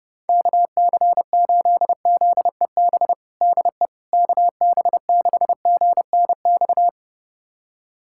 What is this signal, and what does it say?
Morse code, heard as a keyed tone, about 30 wpm, 690 Hz. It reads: KC8ZEB DE KB6GNX
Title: USA Callsign DE Callsign 30wpm